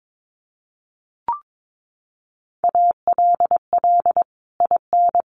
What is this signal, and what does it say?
Morse code, heard as a keyed tone, about 22 wpm, 700 Hz. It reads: ALL IN
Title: Sentences from Top 1500 Words 22wpm